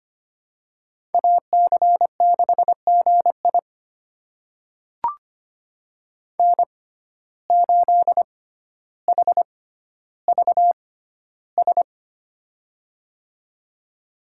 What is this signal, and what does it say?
Morse code, heard as a keyed tone, about 25 wpm, spaced as for 13 wpm, 700 Hz. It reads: AC6GI N8HVS
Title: USA Call Signs - 5 Characters - 6x Character Spacing - 25wpm